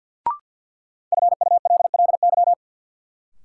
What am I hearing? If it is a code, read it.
Morse code, heard as a keyed tone, about 53 wpm, 700 Hz. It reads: FULLY
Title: Sentences from Top 1300 Words - New Word Drill - 50wpm